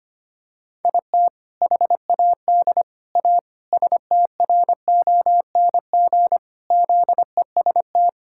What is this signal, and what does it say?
Morse code, heard as a keyed tone, about 25 wpm, 700 Hz. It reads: IT HAD A STRONG ZEST
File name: practice / 25wpm / All-Age Appropriate Jokes - 25wpm